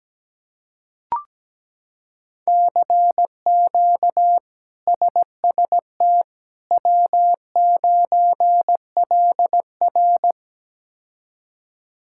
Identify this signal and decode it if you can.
Morse code, heard as a keyed tone, about 17 wpm, 700 Hz. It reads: CQ SST W9LR
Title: K1USN SST QSO Practice - 17wpm